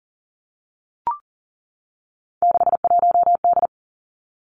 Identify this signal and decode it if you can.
Morse code, heard as a keyed tone, about 40 wpm, 700 Hz. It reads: 61D
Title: Three Challenging Letter-Number Combinations 40wpm